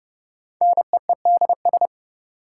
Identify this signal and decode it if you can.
Morse code, heard as a keyed tone, about 29 wpm, 700 Hz. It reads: NEEDS
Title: Sentences from Top 1000 Words - Review Part 3 - New Word Drill - 30wpm